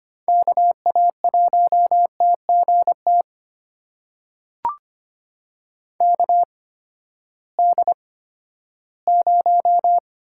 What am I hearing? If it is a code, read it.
Morse code, heard as a keyed tone, about 25 wpm, spaced as for 14 wpm, 700 Hz. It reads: KA1TGT KD0
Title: USA Call Signs - 6 Characters - 8x Character Spacing - 25wpm